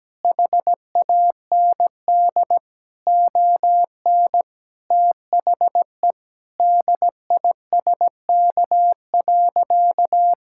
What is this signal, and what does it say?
Morse code, heard as a keyed tone, about 17 wpm, 700 Hz. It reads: HAND ON THE DISK.
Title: Sentences from Top 1000 Words - Review Part 6 - 17wpm